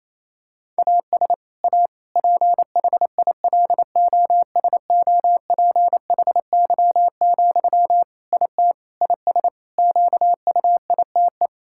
Morse code, heard as a keyed tone, about 28 wpm, 700 Hz. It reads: AS A PHILOSOPHY, IT IS QUITE
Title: Sentences from Top 1000 Words - Review Part 6 - 28wpm